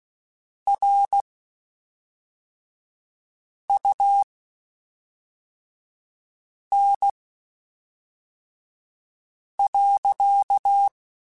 Morse code, heard as a keyed tone, about 16 wpm, spaced as for 3 wpm, 800 Hz. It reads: RUN.
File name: (spaced) / QSOs Spaced-mp3s / kc7sac @5 sp